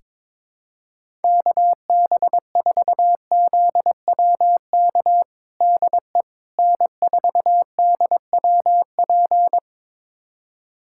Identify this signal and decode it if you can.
Morse code, heard as a keyed tone, about 22 wpm, 700 Hz. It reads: KB4ZWK DE N4DWP
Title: USA Callsign DE Callsign 22wpm